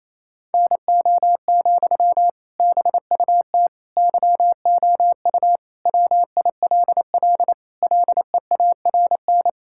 Morse code, heard as a keyed tone, about 28 wpm, 700 Hz. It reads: NO, BUT YOU WILL LEARN
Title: Sentences from Top 1000 Words - Review Part 6 - 28wpm